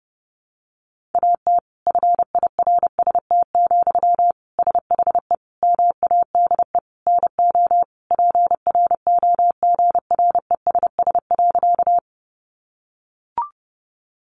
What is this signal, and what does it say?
Morse code, heard as a keyed tone, about 30 wpm, 700 Hz. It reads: AT FIRST, SHE MADE NO PROGRESS.
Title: Sentences from Top 1400 Words 30wpm